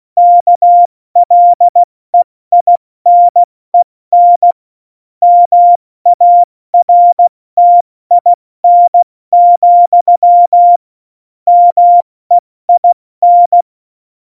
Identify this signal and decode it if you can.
Morse code, heard as a keyed tone, about 16 wpm, 700 Hz. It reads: KLEINEN MARTIN, MEIN
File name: Morsekurs / Witze / Witz 05. Dezember 2017, 70 F80